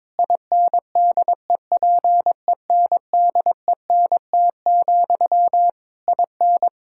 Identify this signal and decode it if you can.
Morse code, heard as a keyed tone, about 22 wpm, 700 Hz. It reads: INDEPENDENT, IN
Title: Sentences from Top 1000 Words - Review Part 6 - 22wpm